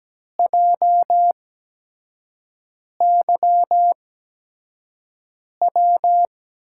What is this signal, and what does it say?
Morse code, heard as a keyed tone, about 17 wpm, spaced as for 5 wpm, 700 Hz. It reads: JYW
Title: USA Call Signs - 8x Character Spacing - 17wpm